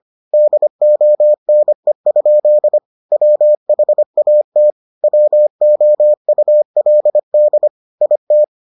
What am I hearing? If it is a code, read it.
Morse code, heard as a keyed tone, about 25 wpm, 600 Hz. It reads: DONE? WHAT WOULD IT